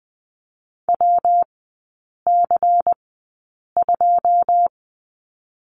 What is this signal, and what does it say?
Morse code, heard as a keyed tone, about 20 wpm, spaced as for 8 wpm, 700 Hz. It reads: WC2
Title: 127 Lesson 21e - Number 6 - Three Characters 20wpm